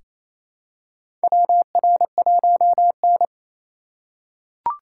Morse code, heard as a keyed tone, about 28 wpm, 700 Hz. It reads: WR1N
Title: USA Call Signs - 4 Characters - 3x Character Spacing - 28wpm